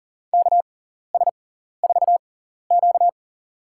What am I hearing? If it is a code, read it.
Morse code, heard as a keyed tone, about 40 wpm, spaced as for 14 wpm, 700 Hz. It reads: KS4Q
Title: USA Call Signs - 6x Character Spacing - 40wpm